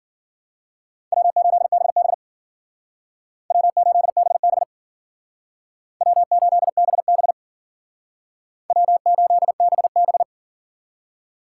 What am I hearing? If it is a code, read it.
Morse code, heard as a keyed tone, about 52 wpm, 700 Hz. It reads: W8BB W8BB W8BB W8BB